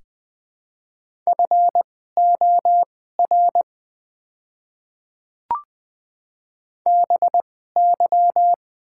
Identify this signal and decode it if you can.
Morse code, heard as a keyed tone, about 20 wpm, spaced as for 14 wpm, 700 Hz. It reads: FOR BY
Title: 2x Character Spacing - Top 100 Words - 20wpm